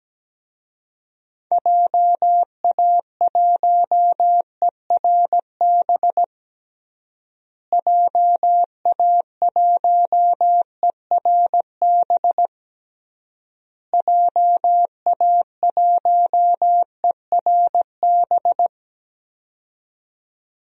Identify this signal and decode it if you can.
Morse code, heard as a keyed tone, about 17 wpm, 700 Hz. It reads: JA1ERB JA1ERB JA1ERB
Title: DXCC-100 Call Signs - 6 Characters - Repeated 3x - 17wpm